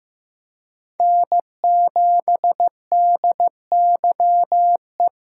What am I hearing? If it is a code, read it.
Morse code, heard as a keyed tone, about 18 wpm, 700 Hz. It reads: N7DYE
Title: Speed Racing - USA 1x3 Call Signs 15wpm